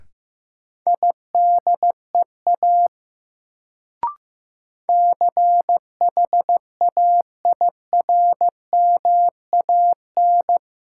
Morse code, heard as a keyed tone, about 15 wpm, 700 Hz. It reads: IDEA CHAIRMAN
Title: Sets of 3 Words - New Word Drill - 15wpm